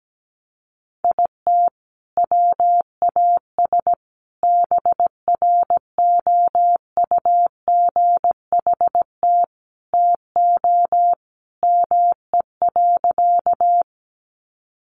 Morse code, heard as a keyed tone, about 17 wpm, 700 Hz. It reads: IT WAS BROUGHT TO ME.
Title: Sentences from Top 500 Words 17wpm